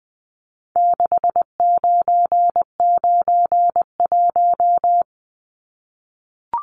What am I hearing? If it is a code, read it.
Morse code, heard as a keyed tone, about 20 wpm, 700 Hz. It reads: 6991
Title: Numbers - 4-Digit 20wpm